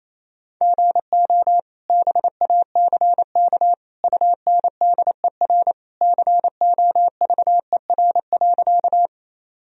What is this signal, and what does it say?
Morse code, heard as a keyed tone, about 28 wpm, 700 Hz. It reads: GO BACK UNDER COVER.
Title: Sentences from Top 1000 Words - Review Part 1 - 28wpm